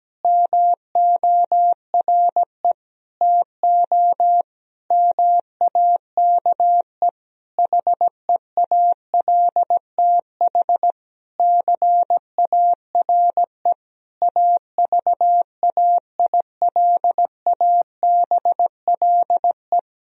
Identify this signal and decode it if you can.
Morse code, heard as a keyed tone, about 17 wpm, 700 Hz. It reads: MORE TO MAKE HEALTH CARE AVAILABLE
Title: Sentences from Top 1000 Words - Review Part 3 - 17wpm